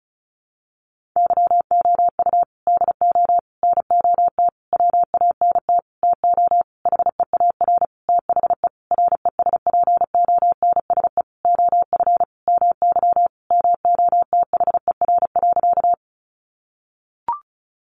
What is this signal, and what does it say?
Morse code, heard as a keyed tone, about 34 wpm, 700 Hz. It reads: YOU DO NOT WANT TO HEAR THE RESPONSE OF MY MOTHER.
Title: Sentences from Top 1400 Words 35wpm